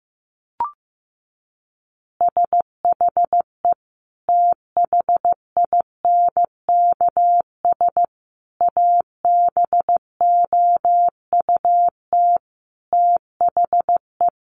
Morse code, heard as a keyed tone, about 15 wpm, 700 Hz. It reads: SHE THINKS ABOUT THE
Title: Easy Phrases 15wpm